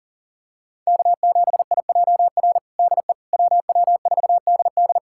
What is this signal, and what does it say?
Morse code, heard as a keyed tone, about 37 wpm, 700 Hz. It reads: K7IJR DE WW4DD
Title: USA Callsign DE Callsign - 5 Characters - 40wpm